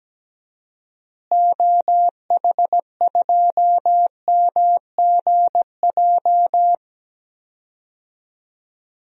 Morse code, heard as a keyed tone, about 17 wpm, 700 Hz. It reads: OH2MGJ
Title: DXCC-100 Call Signs - 6 Characters - Repeated 3x - 17wpm